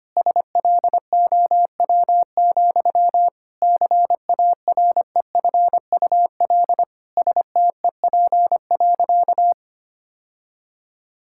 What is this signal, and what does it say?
Morse code, heard as a keyed tone, about 25 wpm, 700 Hz. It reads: SLOW, CAREFUL STEP.
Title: Sentences from Top 1600 Words 25wpm